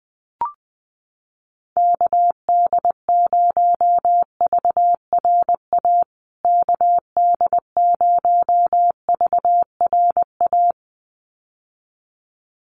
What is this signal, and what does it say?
Morse code, heard as a keyed tone, about 20 wpm, 700 Hz. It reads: KD0VRA KD0VRA
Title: USA Call Signs Repeated 2x 20wpm